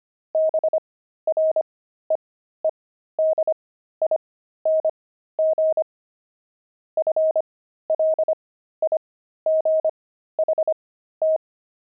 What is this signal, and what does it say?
Morse code, heard as a keyed tone, about 25 wpm, spaced as for 13 wpm, 630 Hz. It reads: BREEDING FLIGHT
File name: pangram-fs-13-cs-25-beehive